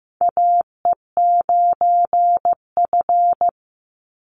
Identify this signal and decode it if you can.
Morse code, heard as a keyed tone, about 15 wpm, 700 Hz. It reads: AE9F
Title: NA SOTA Chaser Callsigns 15wpm